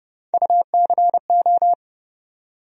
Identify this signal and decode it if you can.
Morse code, heard as a keyed tone, about 30 wpm, 700 Hz. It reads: UCO
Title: USA Call Sign Postfixes - 3 Letters 30wpm